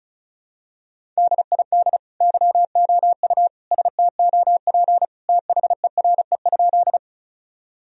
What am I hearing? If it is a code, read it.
Morse code, heard as a keyed tone, about 34 wpm, 700 Hz. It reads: DID YOU STOP THERE?
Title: Sentences from Top 1000 Words - Review Part 1 - 35wpm